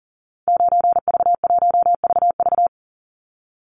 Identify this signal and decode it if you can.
Morse code, heard as a keyed tone, about 40 wpm, 700 Hz. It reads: 9V1VV
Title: Call Signs Repeated 2x - 40wpm